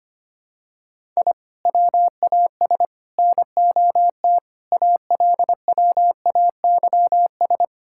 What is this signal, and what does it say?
Morse code, heard as a keyed tone, about 25 wpm, 700 Hz. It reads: I WAS NOT ALWAYS
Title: Sentences from Top 1000 Words - Review Part 6 - 25wpm